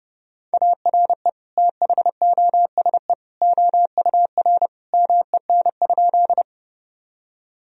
Morse code, heard as a keyed tone, about 32 wpm, 700 Hz. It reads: ARE THOSE OUR MEN?
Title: Sentences from Top 1000 Words - Review Part 1 - 30wpm